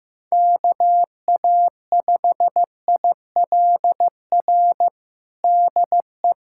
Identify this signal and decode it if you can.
Morse code, heard as a keyed tone, about 15 wpm, 700 Hz. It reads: KA5ILR DE
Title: USA Callsign DE Callsign - 6 Characters - 15wpm